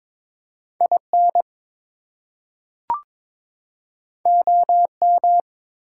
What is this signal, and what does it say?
Morse code, heard as a keyed tone, about 22 wpm, 700 Hz. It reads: IN OM
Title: Top 180 QSO Elements 22wpm